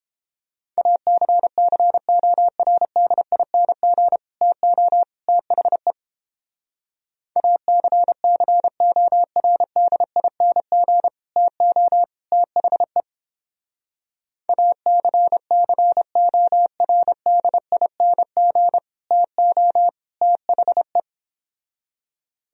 Morse code, heard as a keyed tone, about 29 wpm, 700 Hz. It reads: ACCORDING TO THE ACCORDING TO THE ACCORDING TO THE